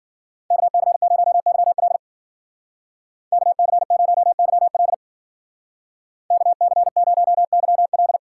Morse code, heard as a keyed tone, about 52 wpm, 700 Hz. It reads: KC0YL KC0YL KC0YL